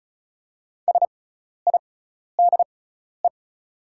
Morse code, heard as a keyed tone, about 36 wpm, spaced as for 12 wpm, 700 Hz. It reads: SIDE
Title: 6x Character Spacing - Top 100 Words - 35wpm